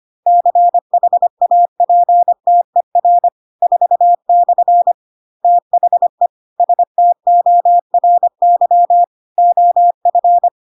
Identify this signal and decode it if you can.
Morse code, heard as a keyed tone, about 25 wpm, 700 Hz. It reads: CHAPTER 4/ THE STORY OF
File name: Pinocchio / Pinocchio ch4 25 25 ik1hge